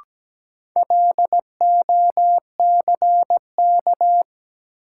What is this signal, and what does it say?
Morse code, heard as a keyed tone, about 17 wpm, 700 Hz. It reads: LOCK
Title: Sentences from Top 1300 Words - New Word Drill - 17wpm